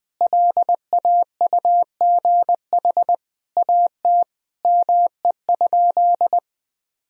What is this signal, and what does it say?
Morse code, heard as a keyed tone, about 20 wpm, 700 Hz. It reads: LAUGH AT ME?
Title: Sentences from Top 1000 Words - Review Part 2 - 20wpm